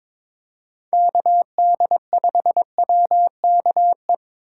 Morse code, heard as a keyed tone, about 22 wpm, 700 Hz. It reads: KD5WKE
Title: USA Call Signs - 6 Characters - 4x Character Spacing - 22wpm